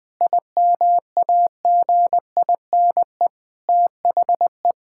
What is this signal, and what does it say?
Morse code, heard as a keyed tone, about 20 wpm, 700 Hz. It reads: IMAGINE THE
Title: Sentences with 7-Word Phrases - 20wpm